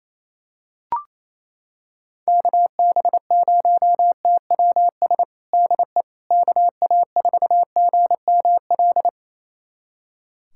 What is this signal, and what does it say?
Morse code, heard as a keyed tone, about 28 wpm, 700 Hz. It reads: KB0TWS DE KA4GML
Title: USA Callsign DE Callsign - 6 Characters - 28wpm